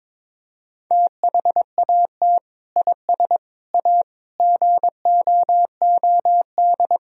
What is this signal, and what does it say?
Morse code, heard as a keyed tone, about 22 wpm, 700 Hz. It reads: THAT IS A GOOD
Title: Sentences from Top 1000 Words - Review Part 4 - 22wpm